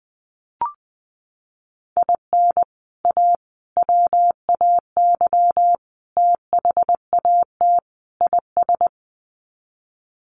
Morse code, heard as a keyed tone, about 20 wpm, 700 Hz. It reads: IN A WAY THAT IS
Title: Sets of 5 Words Encore 20wpm